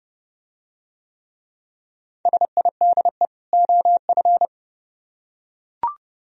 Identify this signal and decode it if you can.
Morse code, heard as a keyed tone, about 28 wpm, 700 Hz. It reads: SIDE OF